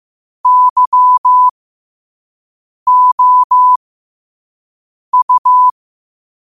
Morse code, heard as a keyed tone, about 15 wpm, spaced as for 5 wpm, 1000 Hz. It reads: YOU
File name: qotdfiles / 5wpm